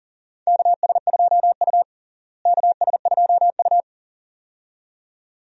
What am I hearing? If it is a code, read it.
Morse code, heard as a keyed tone, about 40 wpm, 700 Hz. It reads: KS2U KS2U
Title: USA Call Signs - 4 Characters - Repeated 2x - 40wpm